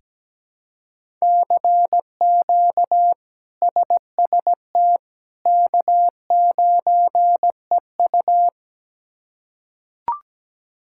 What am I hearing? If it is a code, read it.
Morse code, heard as a keyed tone, about 17 wpm, 700 Hz. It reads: CQ SST K9EU
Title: K1USN SST QSO Practice - 17wpm